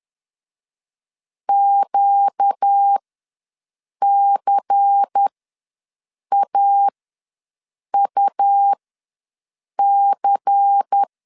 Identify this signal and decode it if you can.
Morse code, heard as a keyed tone, about 10 wpm, spaced as for 6 wpm, 800 Hz. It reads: QCAUC
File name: LEZIONI / 5 LBVC / 5 LBVC ws 10wpm 3733